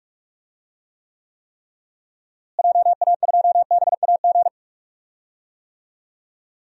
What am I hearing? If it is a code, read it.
Morse code, heard as a keyed tone, about 46 wpm, 700 Hz. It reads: JA2BAG
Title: DXCC-100 Call Signs - 6 Characters - Repeated 3x - 45wpm